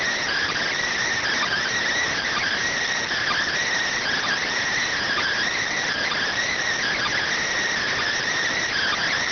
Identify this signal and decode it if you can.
Emergency Alert System (SAME) tones (header bursts), heard as harsh buzzing bursts. It is heard from 0.4 to 8.5 s.